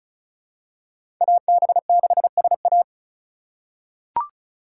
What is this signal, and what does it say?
Morse code, heard as a keyed tone, about 34 wpm, 700 Hz. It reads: AB6SA
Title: USA Call Signs - 5 Characters - 4x Character Spacing - 35wpm